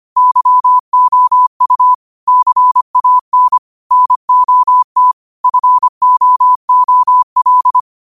Morse code, heard as a keyed tone, about 25 wpm, 1000 Hz. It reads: YOU CAN NOT FOOL
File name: qotdfiles / 25wpm